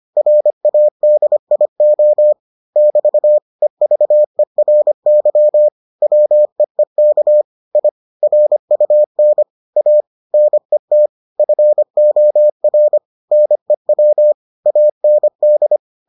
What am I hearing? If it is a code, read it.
Morse code, heard as a keyed tone, about 25 wpm, 600 Hz. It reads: RADIO <BT> EVERY WEEK I RUN A NET FOR NEW AND